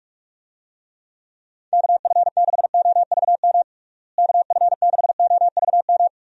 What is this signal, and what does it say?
Morse code, heard as a keyed tone, about 47 wpm, 700 Hz. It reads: KF6OVM KF6OVM
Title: USA Call Signs - 6 Characters - Repeated 2x - 45wpm